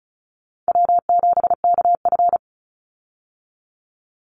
Morse code, heard as a keyed tone, about 34 wpm, 700 Hz. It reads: W7KF
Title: Call Signs Repeated 3x - 35wpm